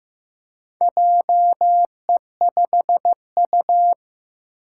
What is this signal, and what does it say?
Morse code, heard as a keyed tone, about 15 wpm, 700 Hz. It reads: JE5U
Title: Four Challenging Letter-Number Combinations 15wpm